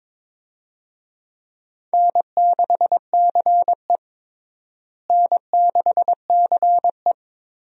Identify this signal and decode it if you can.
Morse code, heard as a keyed tone, about 21 wpm, 700 Hz. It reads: N6CE N6CE
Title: USA Call Signs - 4 Characters - Repeated 3x - 22wpm